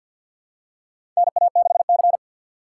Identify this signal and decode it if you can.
Morse code, heard as a keyed tone, about 52 wpm, 700 Hz. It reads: NA6C
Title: USA Call Signs - 4 Characters - 2x Character Spacing - 50wpm